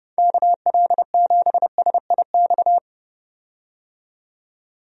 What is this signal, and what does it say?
Morse code, heard as a keyed tone, about 30 wpm, 700 Hz. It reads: KL7SIX
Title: DXCC-100 Call Signs - 6 Characters - 30wpm